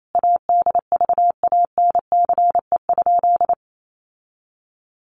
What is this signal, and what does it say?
Morse code, heard as a keyed tone, about 28 wpm, 700 Hz. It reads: ADVANCE?
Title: Sentences from Top 1400 Words 28wpm